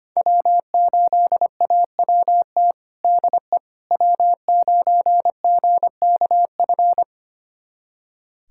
Morse code, heard as a keyed tone, about 25 wpm, 700 Hz. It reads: W8AWT DE W9GKF
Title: USA Callsign DE Callsign - 5 Characters - 25wpm